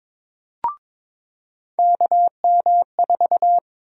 Morse code, heard as a keyed tone, about 22 wpm, 700 Hz. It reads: KM4
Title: USA Call Sign Prefixes - 22wpm